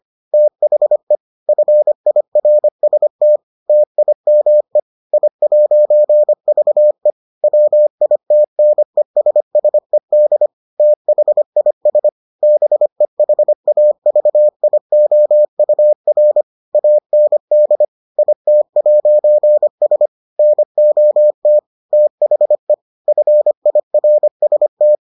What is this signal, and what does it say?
Morse code, heard as a keyed tone, about 25 wpm, 600 Hz. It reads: THE FIRST TIME I'VE WITNESSED THIS BEHAVIOUR AND IT'S NOT THE FIRST